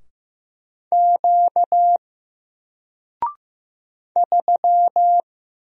Morse code, heard as a keyed tone, about 15 wpm, 700 Hz. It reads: Q 3
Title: Single Letter-Number Alt 15wpm